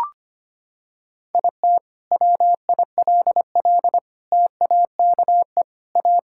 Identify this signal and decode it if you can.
Morse code, heard as a keyed tone, about 25 wpm, 700 Hz. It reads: IT WILL TAKE A
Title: Sentences from Top 1000 Words - Review Part 6 - 25wpm